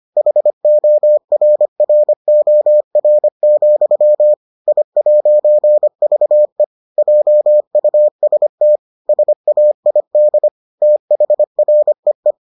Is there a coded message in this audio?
Yes: Morse code from 0.2 to 12.3 s, heard as a keyed tone, about 25 wpm, 600 Hz. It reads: HORROR, I'VE JUST SAID THREE